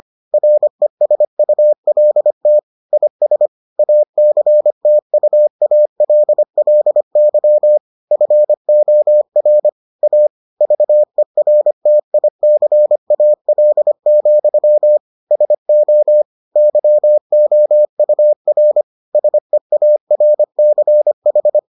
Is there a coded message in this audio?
Yes: Morse code from 0.3 to 21.6 s, heard as a keyed tone, about 25 wpm, 600 Hz. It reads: RESULT IS ACTUALLY FOR A VERTICAL, SO YOUR SEARCH